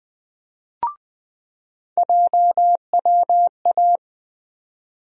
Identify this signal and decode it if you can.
Morse code, heard as a keyed tone, about 20 wpm, 700 Hz. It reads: JWA
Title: USA Call Sign Postfixes - 3 Letters 20wpm